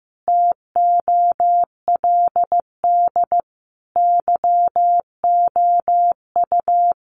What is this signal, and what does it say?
Morse code, heard as a keyed tone, about 15 wpm, 700 Hz. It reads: TOLD YOU
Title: Sentences from Top 500 Words 15wpm